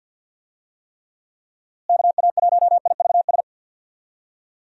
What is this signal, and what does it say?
Morse code, heard as a keyed tone, about 52 wpm, 700 Hz. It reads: KA1IVS
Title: USA Call Signs - 6 Characters - Repeated 2x - 50wpm